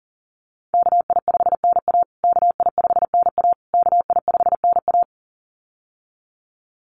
Morse code, heard as a keyed tone, about 40 wpm, 700 Hz. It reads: KI5NA KI5NA KI5NA
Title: USA Call Signs Repeated 3x 40wpm